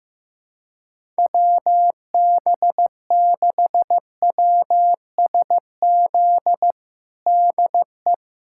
Morse code, heard as a keyed tone, about 15 wpm, 700 Hz. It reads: WB6WSZ DE
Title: USA Callsign DE Callsign 15wpm